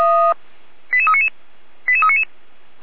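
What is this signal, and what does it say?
Five-tone selective call (ZVEI), heard as quick runs of five tones, tones of about 70 ms. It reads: 88399 88399
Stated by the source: ZVEI five-tone call received on SDR.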